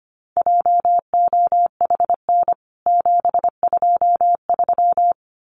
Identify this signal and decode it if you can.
Morse code, heard as a keyed tone, about 25 wpm, 700 Hz. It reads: JOHN 723
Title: CWT Exchange 25wpm